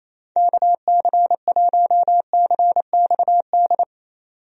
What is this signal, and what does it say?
Morse code, heard as a keyed tone, about 27 wpm, 700 Hz. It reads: KC1CXD
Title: USA Call Signs - 6 Characters - 28wpm